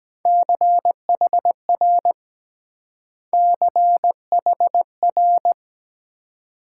Morse code, heard as a keyed tone, about 22 wpm, 700 Hz. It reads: CHR CHR